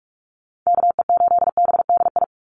Morse code, heard as a keyed tone, about 44 wpm, 700 Hz. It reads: KE8BDI
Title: USA Call Signs 45wpm